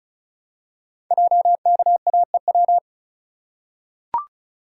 Morse code, heard as a keyed tone, about 36 wpm, 700 Hz. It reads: JKAEW
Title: Code Groups - 5 Letters - 35wpm